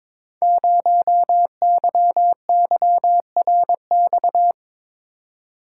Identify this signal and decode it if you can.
Morse code, heard as a keyed tone, about 22 wpm, 700 Hz. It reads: 0YYRX
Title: Code Groups - 5 Letter:Numbers - 22wpm